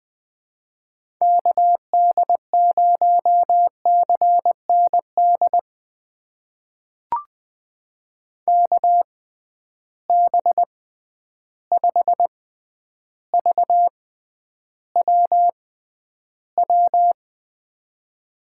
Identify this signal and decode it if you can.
Morse code, heard as a keyed tone, about 20 wpm, spaced as for 10 wpm, 700 Hz. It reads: KD0CND KB5VWW
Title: USA Call Signs - 6 Characters - 6x Character Spacing - 20wpm